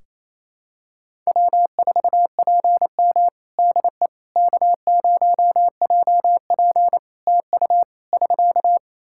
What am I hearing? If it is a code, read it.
Morse code, heard as a keyed tone, about 27 wpm, 700 Hz. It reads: W4PM DE K0JP TU <SK>
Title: Practice QSO - Exchange 4 - Ending - 28wpm